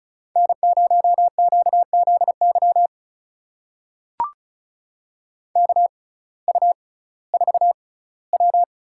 Morse code, heard as a keyed tone, about 36 wpm, spaced as for 19 wpm, 700 Hz. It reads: N0QZY KU4W
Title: USA Call Signs - 5 Characters - 6x Character Spacing - 35wpm